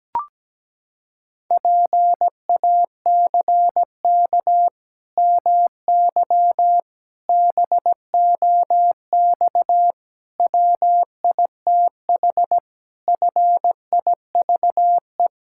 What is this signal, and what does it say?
Morse code, heard as a keyed tone, about 17 wpm, 700 Hz. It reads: PACK MY BOX WITH FIVE
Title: Pangram Challenge v2 - 17wpm